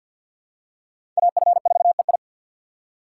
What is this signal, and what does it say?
Morse code, heard as a keyed tone, about 46 wpm, 700 Hz. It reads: AF4EI
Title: USA Call Signs - 5 Characters - 2x Character Spacing - 50wpm